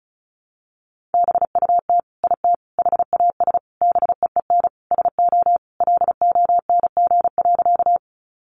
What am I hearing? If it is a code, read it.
Morse code, heard as a keyed tone, about 34 wpm, 700 Hz. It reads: BUT IT HAS BEEN SO LONG.
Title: Sentences from Top 100 Words 35wpm